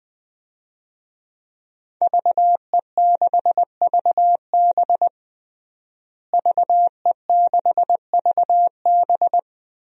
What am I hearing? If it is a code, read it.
Morse code, heard as a keyed tone, about 20 wpm, 700 Hz. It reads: VE6VB VE6VB
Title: DXCC-100 Call Signs - 5 Characters - Repeated 3x - 20wpm